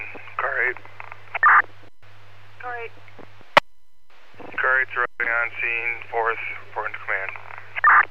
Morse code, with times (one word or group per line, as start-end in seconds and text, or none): none